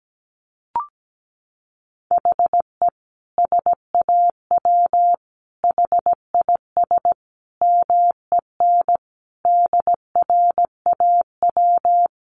Morse code, heard as a keyed tone, about 17 wpm, 700 Hz. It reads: HE SAW HIS MEN DRAW
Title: Sentences from Top 300 Words 17wpm